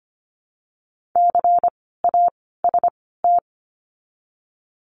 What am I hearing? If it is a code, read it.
Morse code, heard as a keyed tone, about 25 wpm, spaced as for 16 wpm, 700 Hz. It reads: CAST
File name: course / 25:10wpm / 094 Lesson 16f - Letter c - Words 25wpm